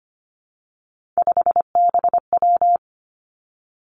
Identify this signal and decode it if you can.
Morse code, heard as a keyed tone, about 25 wpm, 700 Hz. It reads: 5BW
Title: Three Challenging Letter-Number Combinations 25wpm